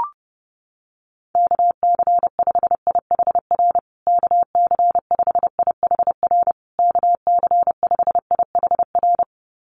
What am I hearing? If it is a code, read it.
Morse code, heard as a keyed tone, about 30 wpm, 700 Hz. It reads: KC5IHR KC5IHR KC5IHR
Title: USA Call Signs Repeated 3x 30wpm